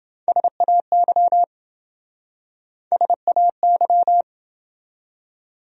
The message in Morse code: SAY SAY

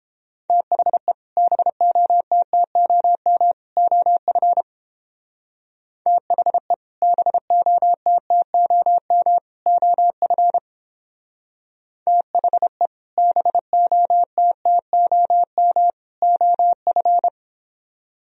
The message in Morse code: THE BOTTOM OF THE BOTTOM OF THE BOTTOM OF

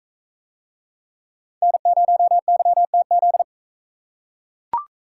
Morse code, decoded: N0YTZ

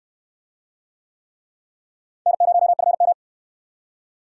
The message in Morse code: N0FM

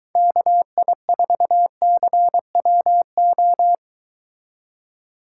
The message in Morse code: KI4CWO

